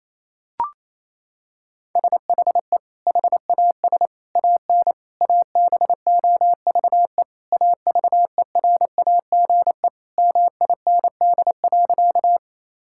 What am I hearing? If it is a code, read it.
Morse code, heard as a keyed tone, about 27 wpm, 700 Hz. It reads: SHE HAS AN ABOVE AVERAGE MIND.
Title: Sentences from Top 1000 Words - Review Part 6 - 28wpm